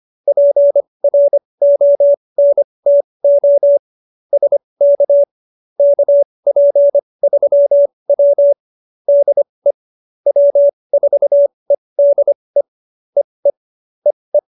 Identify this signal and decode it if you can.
Morse code, heard as a keyed tone, about 25 wpm, spaced as for 20 wpm, 570 Hz. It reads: PRONTO SK KP3W DE W4EDE EE EE